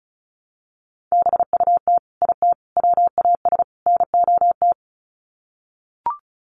Morse code, heard as a keyed tone, about 34 wpm, 700 Hz. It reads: BUT IT WAS NOT
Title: Sets of 4 Words 35wpm